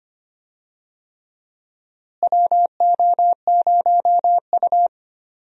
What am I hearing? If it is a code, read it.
Morse code, heard as a keyed tone, about 25 wpm, 700 Hz. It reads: WO0U
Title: USA Call Signs - 4 Characters - Repeated 3x - 25wpm